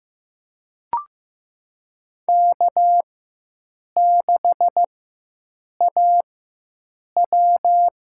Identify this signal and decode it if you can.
Morse code, heard as a keyed tone, about 15 wpm, spaced as for 7 wpm, 700 Hz. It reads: K6AW
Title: USA Call Signs - 4x Character Spacing - 15wpm